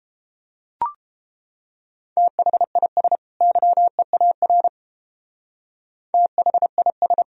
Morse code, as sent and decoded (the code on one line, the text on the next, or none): - .... .. ... / -.-- . .- .-. / - .... .. ...
THIS YEAR THIS